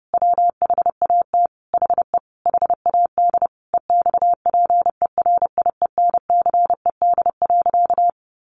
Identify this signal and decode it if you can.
Morse code, heard as a keyed tone, about 30 wpm, 700 Hz. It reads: WHAT HE HAD EXPERIENCED.
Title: Sentences from Top 1100 Words 30wpm